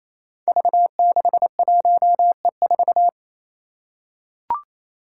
Morse code, heard as a keyed tone, about 28 wpm, 700 Hz. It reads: V61E4
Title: Five Challenging Letter-Number Combinations 28wpm